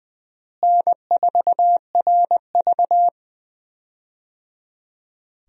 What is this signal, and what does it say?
Morse code, heard as a keyed tone, about 20 wpm, 700 Hz. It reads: N4RV
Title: USA Call Signs - 4 Characters - Repeated 3x - 20wpm